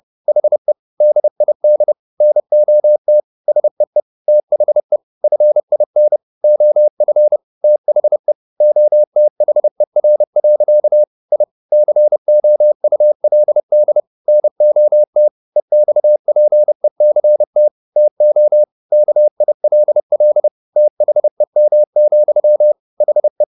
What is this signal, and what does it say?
Morse code, heard as a keyed tone, about 30 wpm, 600 Hz. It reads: HE DID NOT SEE THE FIN OF THE OTHER. I COULD NOT EXPECT TO KILL THEM, HE